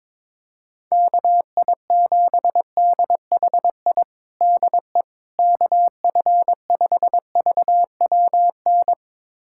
Morse code, heard as a keyed tone, about 22 wpm, 700 Hz. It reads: KI7DHI DE KF5VWN
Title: USA Callsign DE Callsign - 6 Characters - 22wpm